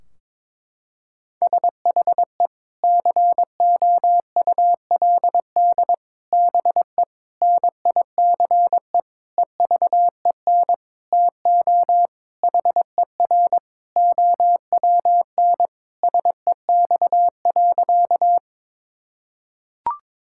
Morse code, heard as a keyed tone, about 22 wpm, 700 Hz. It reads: SHE COULD BE NICE EVEN TO HER OWN SEX.
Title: Sentences from Top 1600 Words 22wpm